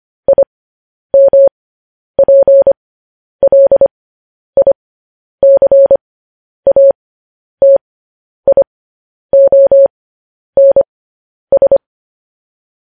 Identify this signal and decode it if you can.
Morse code, heard as a keyed tone, about 25 wpm, spaced as for 10 wpm, 550 Hz. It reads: IMPLICATIONS